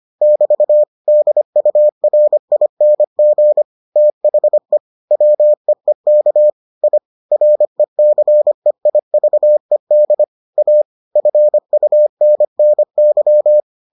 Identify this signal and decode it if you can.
Morse code, heard as a keyed tone, about 25 wpm, 600 Hz. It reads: <BT> DURING THE WEEK I RECEIVED A FUNNY